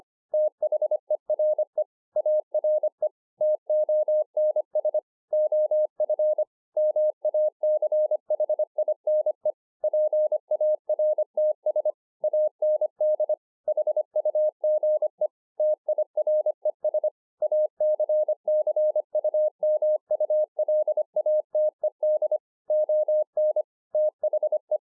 Morse code, heard as a keyed tone, about 25 wpm, 610 Hz. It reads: THERE ARE TONS OF MACHINE PARTS AND HUGE TIRES ACCUMULATED ON THE